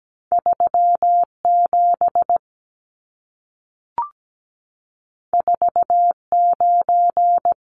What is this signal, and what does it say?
Morse code, heard as a keyed tone, about 17 wpm, 700 Hz. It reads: 37 49
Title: Numbers - 2-Digit 17wpm